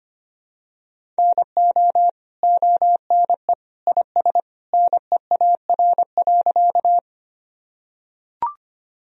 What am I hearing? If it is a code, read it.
Morse code, heard as a keyed tone, about 25 wpm, 700 Hz. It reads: NO ONE IS NEAR.
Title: Sentences from Top 1000 Words - Review Part 1 - 25wpm